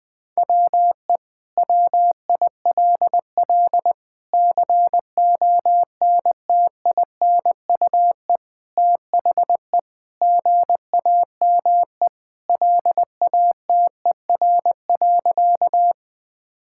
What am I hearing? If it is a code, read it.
Morse code, heard as a keyed tone, about 20 wpm, 700 Hz. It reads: WE WILL CONTINUE THE GAME LATER.
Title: Sentences from Top 1000 Words - Review Part 6 - 20wpm